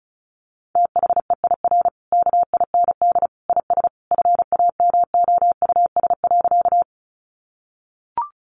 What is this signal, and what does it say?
Morse code, heard as a keyed tone, about 34 wpm, 700 Hz. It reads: THEIR KIND IS FAMOUS.
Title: Sentences from Top 1500 Words 35wpm